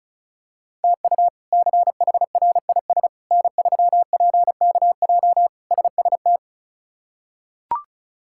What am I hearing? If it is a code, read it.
Morse code, heard as a keyed tone, about 36 wpm, 700 Hz. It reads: TU CHRIS N3PKJ SST
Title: K1USN SST QSO Practice - 35wpm